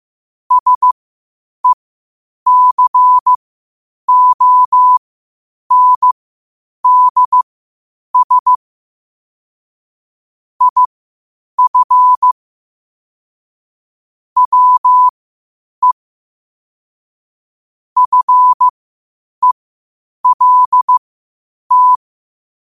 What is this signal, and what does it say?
Morse code, heard as a keyed tone, about 15 wpm, spaced as for 8 wpm, 1000 Hz. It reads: SECONDS IF WE FELT
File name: qotdfiles / 8wpm